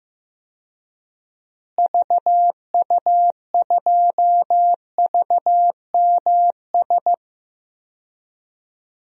Morse code, heard as a keyed tone, about 15 wpm, 700 Hz. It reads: VU2VMS
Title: DXCC-100 Call Signs - 6 Characters - Repeated 2x - 15wpm